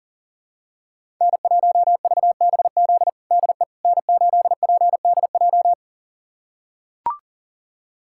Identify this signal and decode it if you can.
Morse code, heard as a keyed tone, about 40 wpm, 700 Hz. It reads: N1VBZ DE N8PDJ
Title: USA Callsign DE Callsign - 5 Characters - 40wpm